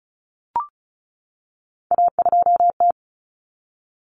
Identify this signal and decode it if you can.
Morse code, heard as a keyed tone, about 34 wpm, 700 Hz. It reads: A2T
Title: Common Cut Numbers - 3-Digit 35wpm